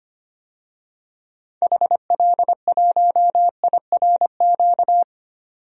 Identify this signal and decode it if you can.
Morse code, heard as a keyed tone, about 25 wpm, 700 Hz. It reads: HL1IRQ
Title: DXCC-100 Call Signs - 6 Characters - 25wpm